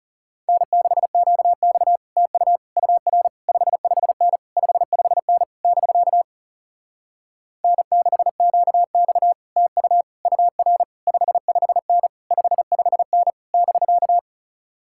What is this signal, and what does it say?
Morse code, heard as a keyed tone, about 37 wpm, 700 Hz. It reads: N6QX TU UR 55N 55N <BK> N6QX TU UR 55N 55N <BK>
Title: Speed-Racing Common QSO Phrases Encore - 30wpm